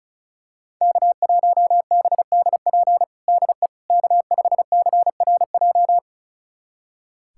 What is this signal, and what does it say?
Morse code, heard as a keyed tone, about 36 wpm, 700 Hz. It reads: K1BDP DE K5CRJ
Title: USA Callsign DE Callsign - 5 Characters - 35wpm